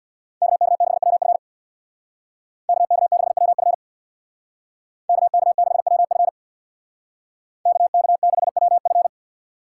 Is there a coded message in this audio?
Yes: Morse code from 0.4 to 9.1 s, heard as a keyed tone, about 64 wpm, 700 Hz. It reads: KK6WF KK6WF KK6WF KK6WF